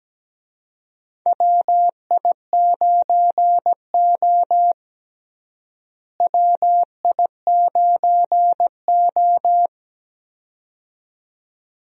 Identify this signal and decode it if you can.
Morse code, heard as a keyed tone, about 17 wpm, 700 Hz. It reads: WI9O WI9O
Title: USA Call Signs - 4 Characters - Repeated 2x - 17wpm